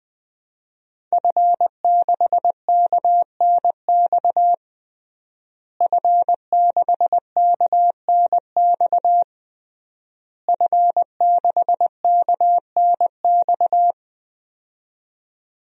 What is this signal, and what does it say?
Morse code, heard as a keyed tone, about 20 wpm, 700 Hz. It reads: F6KNX F6KNX F6KNX
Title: DXCC-100 Call Signs - 5 Characters - Repeated 3x - 20wpm